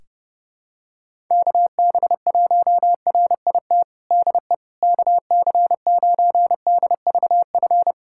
Morse code, heard as a keyed tone, about 30 wpm, 700 Hz. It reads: KB1RIT DE KC9DVF
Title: USA Callsign DE Callsign - 6 Characters - 30wpm